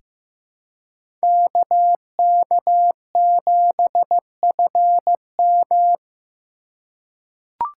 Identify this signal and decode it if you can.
Morse code, heard as a keyed tone, about 15 wpm, 700 Hz. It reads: KK7FM
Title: USA Call Signs - 5 Characters - 4x Character Spacing - 15wpm